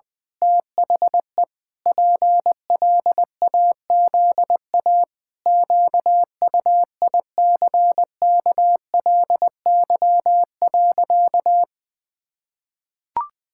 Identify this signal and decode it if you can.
Morse code, heard as a keyed tone, about 20 wpm, 700 Hz. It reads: THE PLAZA QUICKLY.
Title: Pangram Challenge v2 - 20wpm